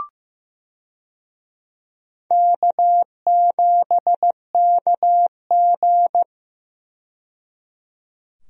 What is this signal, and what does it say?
Morse code, heard as a keyed tone, about 15 wpm, 700 Hz. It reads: K7KG
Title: USA Call Signs - 4 Characters - 15wpm